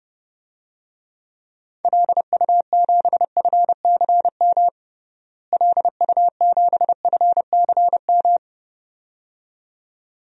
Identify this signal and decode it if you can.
Morse code, heard as a keyed tone, about 30 wpm, 700 Hz. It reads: LU7FCM LU7FCM
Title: DXCC-100 Call Signs - 6 Characters - Repeated 2x - 30wpm